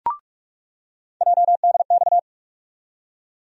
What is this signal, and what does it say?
Morse code, heard as a keyed tone, about 46 wpm, 700 Hz. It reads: JDX
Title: USA Call Sign Postfixes - 3 Letters 45wpm